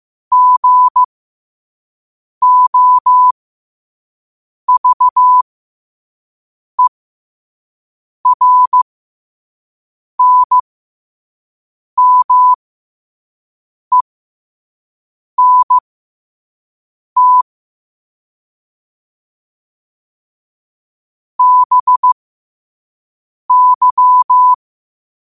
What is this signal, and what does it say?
Morse code, heard as a keyed tone, about 15 wpm, spaced as for 5 wpm, 1000 Hz. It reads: GOVERNMENT BY
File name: qotdfiles / 5wpm